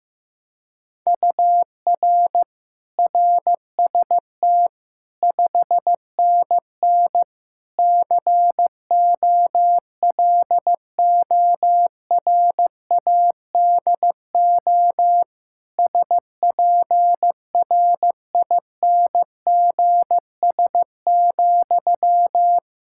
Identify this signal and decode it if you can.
Morse code, heard as a keyed tone, about 15 wpm, 700 Hz. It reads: UR RST 5NN COLORADO SPRINGS,
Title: Practice QSO - Exchange 1 - Beginning - 15wpm